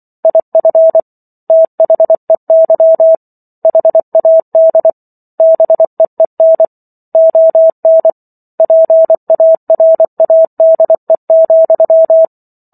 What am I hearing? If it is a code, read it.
Morse code, heard as a keyed tone, about 24 wpm, 650 Hz. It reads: IF THEY HAD BEEN ON PARADE,